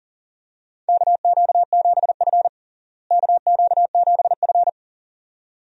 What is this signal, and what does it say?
Morse code, heard as a keyed tone, about 40 wpm, 700 Hz. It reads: KQ7F KQ7F
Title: USA Call Signs - 4 Characters - Repeated 2x - 40wpm